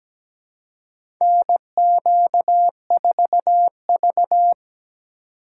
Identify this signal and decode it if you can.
Morse code, heard as a keyed tone, about 17 wpm, 700 Hz. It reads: NQ4V
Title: USA Call Signs - 4 Characters - 3x Character Spacing - 17wpm